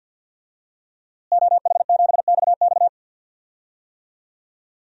Morse code, heard as a keyed tone, about 47 wpm, 700 Hz. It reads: OH7XX
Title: DXCC-100 Call Signs - 5 Characters - Repeated 2x - 50wpm